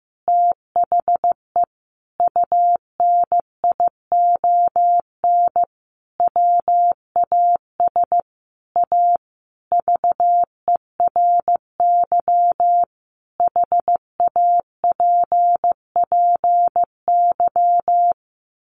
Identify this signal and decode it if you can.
Morse code, heard as a keyed tone, about 15 wpm, 700 Hz. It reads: THE UNION WAS A VERY HAPPY
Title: Sentences from Top 1400 Words 15wpm